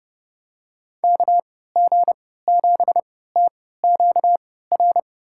Morse code, heard as a keyed tone, about 30 wpm, spaced as for 17 wpm, 700 Hz. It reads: KG7TQR
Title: USA Call Signs - 6 Characters - 3x Character Spacing - 30wpm